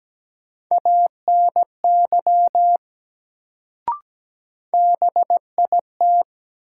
Morse code, heard as a keyed tone, about 17 wpm, 700 Hz. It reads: ANY BIT
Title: Sets of 3 Words - New Word Drill - 17wpm